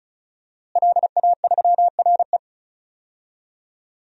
Morse code, heard as a keyed tone, about 36 wpm, 700 Hz. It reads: LA3RE
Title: DXCC-100 Call Signs - 5 Characters - Repeated 2x - 35wpm